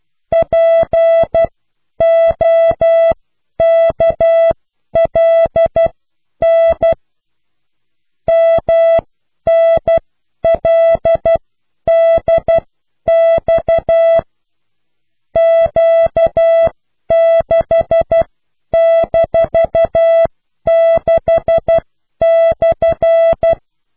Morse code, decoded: POKLN MNLDX Q6-6/